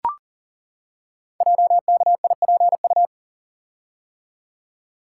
Morse code, heard as a keyed tone, about 40 wpm, 690 Hz. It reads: JKIPU
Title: Code Groups - 5 Letters - 40wpm